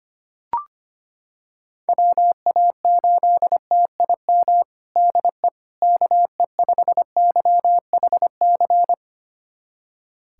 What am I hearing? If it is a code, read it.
Morse code, heard as a keyed tone, about 25 wpm, 700 Hz. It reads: WA8TIM DE KE5YHC
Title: USA Callsign DE Callsign 25wpm